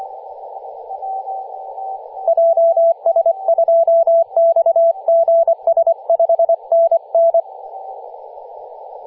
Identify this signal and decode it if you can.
Morse code, heard as a keyed tone, about 24 wpm, 670 Hz, with radio noise. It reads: JS2XGS 5NN
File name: posts / 2016-nov-10-0528z-t30ma-21-007mhz-cw